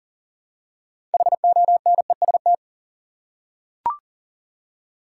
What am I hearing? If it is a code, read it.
Morse code, heard as a keyed tone, about 40 wpm, 700 Hz. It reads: HONEST
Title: Six Letter Words - 40wpm